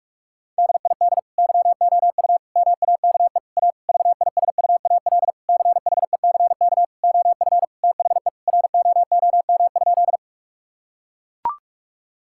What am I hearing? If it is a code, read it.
Morse code, heard as a keyed tone, about 42 wpm, 700 Hz. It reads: DID YOU MAKE A VISUAL CHECK OF THE ROOM?
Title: Sentences from Top 1000 Words - Review Part 6 - 45wpm